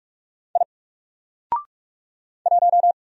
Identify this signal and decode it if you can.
Morse code, heard as a keyed tone, about 44 wpm, 700 Hz. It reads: I 1
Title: Single Letter-Number Alt 45wpm